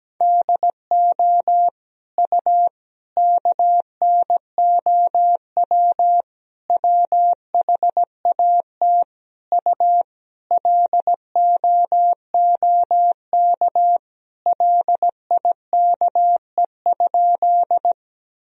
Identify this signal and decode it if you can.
Morse code, heard as a keyed tone, about 17 wpm, 700 Hz. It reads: DO U KNOW WHAT U LOOK LIKE?
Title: Sentences from Top 100 Words and Abbreviations 17wpm